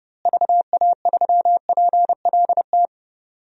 Morse code, heard as a keyed tone, about 32 wpm, 700 Hz. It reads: VA3PLT
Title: DXCC-100 Call Signs - 6 Characters - Repeated 3x - 30wpm